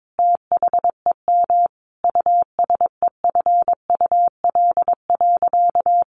Morse code, heard as a keyed tone, about 22 wpm, 700 Hz. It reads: THEM USEFUL.
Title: Sentences from Top 700 Words 22wpm